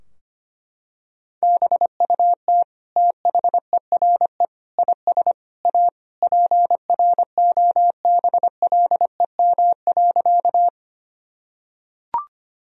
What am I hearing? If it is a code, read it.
Morse code, heard as a keyed tone, about 25 wpm, 700 Hz. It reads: BUT THERE IS A PROBLEM.
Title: Sentences from Top 1000 Words - Review Part 2 - 25wpm